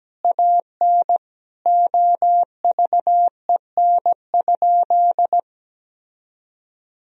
Morse code, heard as a keyed tone, about 17 wpm, 700 Hz. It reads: AN OVEN?
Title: Sentences from Top 1000 Words - Review Part 4 - 17wpm